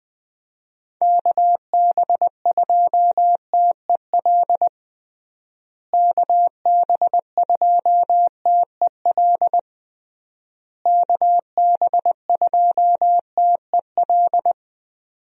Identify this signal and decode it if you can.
Morse code, heard as a keyed tone, about 20 wpm, 700 Hz. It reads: KB2TEL KB2TEL KB2TEL
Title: USA Call Signs - 6 Characters - Repeated 3x - 20wpm